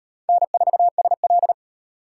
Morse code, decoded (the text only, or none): N4SL